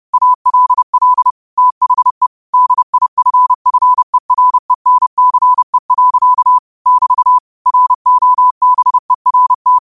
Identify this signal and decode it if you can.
Morse code, heard as a keyed tone, about 30 wpm, 1000 Hz. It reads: ALL THE DIFFERENCE. <BT> ROBERT
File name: qotdfiles / 30wpm